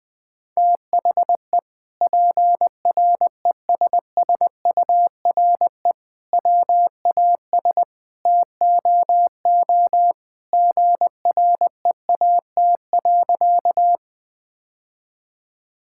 Morse code, heard as a keyed tone, about 20 wpm, 700 Hz. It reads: THE PRESSURE WAS TOO GREAT.
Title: Sentences from Top 1000 Words - Review Part 6 - 20wpm